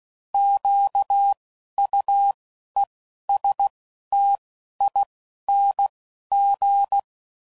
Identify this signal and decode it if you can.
Morse code, heard as a keyed tone, about 16 wpm, 800 Hz. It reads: Q U E S T I N G
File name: Prefixes / By First Letter / Qs